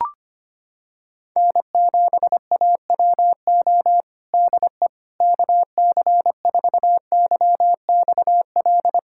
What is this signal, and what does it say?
Morse code, heard as a keyed tone, about 25 wpm, 700 Hz. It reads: N7AWO DE KC4YXL
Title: USA Callsign DE Callsign 25wpm